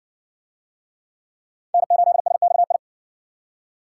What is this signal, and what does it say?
Morse code, heard as a keyed tone, about 58 wpm, 700 Hz. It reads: N8SXI